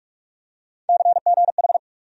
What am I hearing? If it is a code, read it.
Morse code, heard as a keyed tone, about 46 wpm, 700 Hz. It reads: CGH
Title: USA Call Sign Postfixes - 3 Letters 45wpm